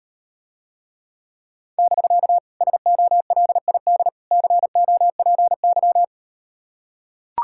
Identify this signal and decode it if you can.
Morse code, heard as a keyed tone, about 38 wpm, 700 Hz. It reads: <BK> SOLID COPY